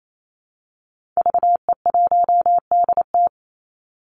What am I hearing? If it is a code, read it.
Morse code, heard as a keyed tone, about 28 wpm, 700 Hz. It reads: VE1DT
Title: Call Signs Repeated 3x - 28wpm